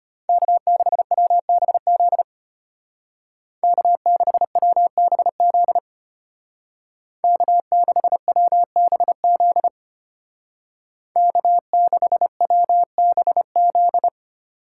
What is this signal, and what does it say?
Morse code, heard as a keyed tone, about 30 wpm, 700 Hz. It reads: K6WBZ K6WBZ K6WBZ K6WBZ
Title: Speed Racing - USA Call Signs 25wpm